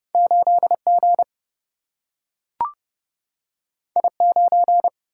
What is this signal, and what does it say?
Morse code, heard as a keyed tone, about 30 wpm, 700 Hz. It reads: 8G I9
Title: Single-Letter Single-Number 30wpm